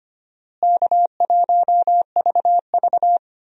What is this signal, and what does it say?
Morse code, heard as a keyed tone, about 25 wpm, 700 Hz. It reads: K1VV
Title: USA Call Signs - 4 Characters - Repeated 3x - 25wpm